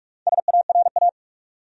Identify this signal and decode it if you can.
Morse code, heard as a keyed tone, about 44 wpm, 700 Hz. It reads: SARA
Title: USA Common Names - 45wpm